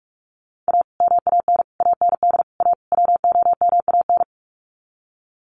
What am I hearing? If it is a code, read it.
Morse code, heard as a keyed tone, about 44 wpm, 700 Hz. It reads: A MAN AND A WOMAN
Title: Sets of 5 Words 45wpm